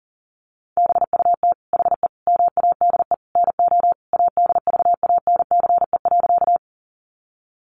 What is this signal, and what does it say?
Morse code, heard as a keyed tone, about 40 wpm, 700 Hz. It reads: BUT HE MADE NO ADVANCE.
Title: Sentences from Top 1400 Words 40wpm